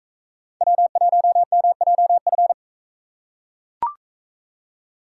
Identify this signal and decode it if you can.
Morse code, heard as a keyed tone, about 39 wpm, 700 Hz. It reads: W1MJF